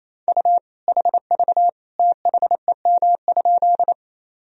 Morse code, heard as a keyed tone, about 28 wpm, 700 Hz. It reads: U HV THEM?
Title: Sentences from Top 100 Words and Abbreviations 28wpm